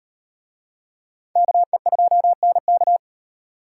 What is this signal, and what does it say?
Morse code, heard as a keyed tone, about 38 wpm, 700 Hz. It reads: KE2NK